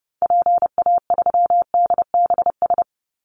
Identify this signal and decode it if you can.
Morse code, heard as a keyed tone, about 30 wpm, 700 Hz. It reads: PA3DBS
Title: Call Signs - 30wpm